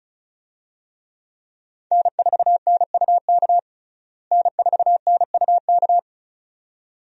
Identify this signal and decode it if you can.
Morse code, heard as a keyed tone, about 36 wpm, 700 Hz. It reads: N4NUK N4NUK
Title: USA Call Signs - 5 Characters - Repeated 2x - 35wpm